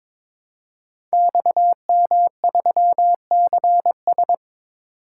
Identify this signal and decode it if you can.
Morse code, heard as a keyed tone, about 23 wpm, 700 Hz. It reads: XM3CS
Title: Code Groups - 5 Letter:Numbers - 22wpm